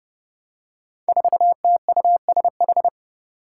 Morse code, heard as a keyed tone, about 32 wpm, 700 Hz. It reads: 4TUSH
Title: Five Challenging Letter-Number Combinations 30wpm